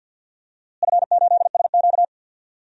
Morse code, heard as a keyed tone, about 52 wpm, 700 Hz. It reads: F8SQ
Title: DXCC-100 Call Signs - 4 Characters - Repeated 2x - 50wpm